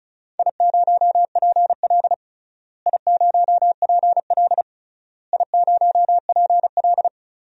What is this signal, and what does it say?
Morse code, heard as a keyed tone, about 36 wpm, 700 Hz. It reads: I0PL I0PL I0PL
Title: DXCC-100 Call Signs - 4 Characters - Repeated 3x - 35wpm